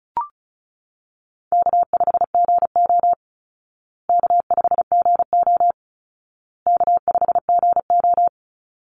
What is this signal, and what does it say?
Morse code, heard as a keyed tone, about 34 wpm, 700 Hz. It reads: K5GO K5GO K5GO
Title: Easy Call Signs Repeated 3x - 35wpm